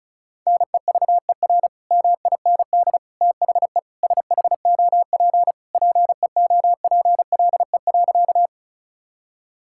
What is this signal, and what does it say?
Morse code, heard as a keyed tone, about 36 wpm, 700 Hz. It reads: NEVER MIND THE SHOP PEOPLE.
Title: Sentences from Top 1600 Words 35wpm